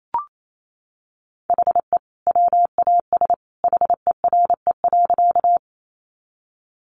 Morse code, heard as a keyed tone, about 28 wpm, 700 Hz. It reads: HE WAS HERE.
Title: Sentences from Top 200 Words 28wpm